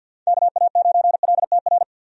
Morse code, heard as a keyed tone, about 51 wpm, 700 Hz. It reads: KA9LTR
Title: USA Call Signs - 6 Characters - 50wpm